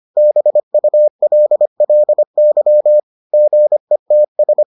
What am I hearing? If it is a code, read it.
Morse code, heard as a keyed tone, about 25 wpm, 600 Hz. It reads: BULLY GETS